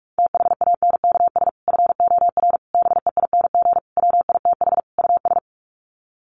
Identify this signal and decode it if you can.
Morse code, heard as a keyed tone, about 44 wpm, 700 Hz. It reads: THANKS FOR BEING WITH US
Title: Sets of 5 Words 45wpm